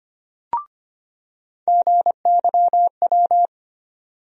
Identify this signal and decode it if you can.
Morse code, heard as a keyed tone, about 25 wpm, 700 Hz. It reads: GYW
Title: USA Call Sign Postfixes - 3 Letters 25wpm